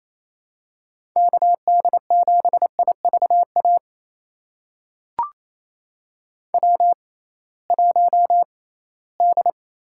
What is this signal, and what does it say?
Morse code, heard as a keyed tone, about 28 wpm, spaced as for 18 wpm, 700 Hz. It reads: KD7IVA W1D
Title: USA Call Signs - 6x Character Spacing - 28wpm